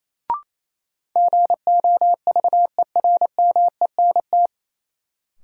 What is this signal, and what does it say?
Morse code, heard as a keyed tone, about 28 wpm, 700 Hz. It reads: GOVERMENT
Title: Sentences from Top 1300 Words - New Word Drill - 28wpm